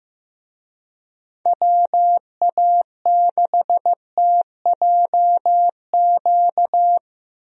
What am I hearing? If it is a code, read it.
Morse code, heard as a keyed tone, about 15 wpm, 700 Hz. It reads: WA6TJQ
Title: USA Call Signs - 6 Characters - Repeated 3x - 15wpm